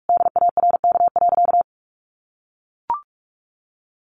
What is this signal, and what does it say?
Morse code, heard as a keyed tone, about 44 wpm, 700 Hz. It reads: DARK.
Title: Sentences from Top 600 Words 45wpm